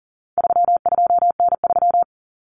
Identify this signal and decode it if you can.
Morse code, heard as a keyed tone, about 40 wpm, 700 Hz. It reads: 32N3
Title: Common Cut Numbers - 4-Digit 40wpm